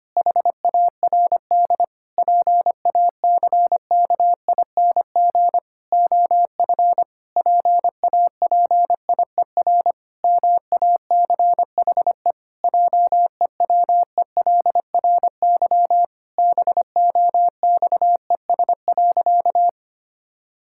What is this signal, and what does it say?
Morse code, heard as a keyed tone, about 25 wpm, 700 Hz. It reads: HARD PACKING OF PAPIER MACHE JEWELRY BOXES.
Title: Pangram Challenge v2 - 25wpm